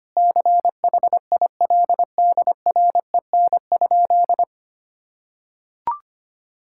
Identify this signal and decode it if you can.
Morse code, heard as a keyed tone, about 25 wpm, 700 Hz. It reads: CHILDREN?
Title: Sentences from Top 1000 Words - Review Part 6 - 25wpm